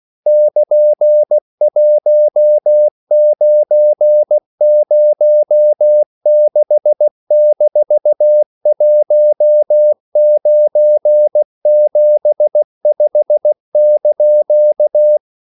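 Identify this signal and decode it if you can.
Morse code, heard as a keyed tone, about 16 wpm, 600 Hz. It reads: <KN>1906-1975)
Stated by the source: English quotations sent in CW as practice at 16 words per minute.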